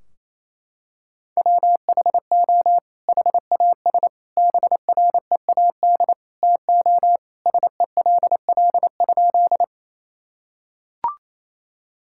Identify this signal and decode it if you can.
Morse code, heard as a keyed tone, about 27 wpm, 700 Hz. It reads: WHO HAS BREAD TO SELL?
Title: Sentences from Top 1600 Words 28wpm